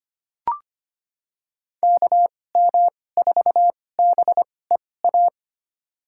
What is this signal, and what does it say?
Morse code, heard as a keyed tone, about 25 wpm, spaced as for 18 wpm, 700 Hz. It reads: KM4BEA
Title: USA Call Signs - 2x Character Spacing - 25wpm